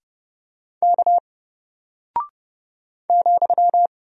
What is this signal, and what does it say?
Morse code, heard as a keyed tone, about 30 wpm, 700 Hz. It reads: K ,
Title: Single Character (Letter:Number:Punctuation:Prosign) Alt 30wpm